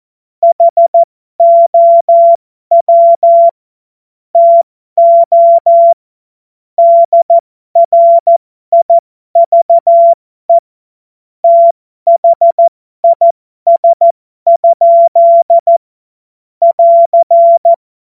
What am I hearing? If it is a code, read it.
Morse code, heard as a keyed tone, about 14 wpm, spaced as for 12 wpm, 680 Hz. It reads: HOW TO DRIVE THIS? <AR>
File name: Morsekurs / Witze